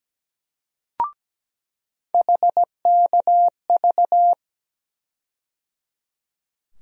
Morse code, heard as a keyed tone, about 17 wpm, 700 Hz. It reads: HKV
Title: USA Call Sign Postfixes - 3 Letters 17wpm